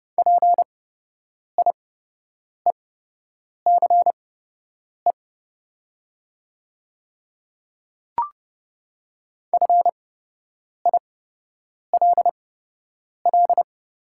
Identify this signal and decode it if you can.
Morse code, heard as a keyed tone, about 30 wpm, spaced as for 8 wpm, 690 Hz. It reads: PIECE FILL
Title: 8x Character Spacing - Top 500 Words - 30wpm